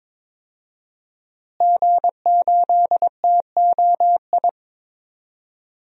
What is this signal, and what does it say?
Morse code, heard as a keyed tone, about 21 wpm, 700 Hz. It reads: G8TOI
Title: DXCC-100 Call Signs - 5 Characters - Repeated 3x - 22wpm